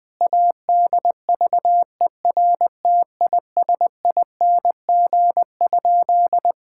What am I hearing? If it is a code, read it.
Morse code, heard as a keyed tone, about 20 wpm, 700 Hz. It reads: ADVERTISING?
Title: Sentences from Top 1000 Words - Review Part 6 - 20wpm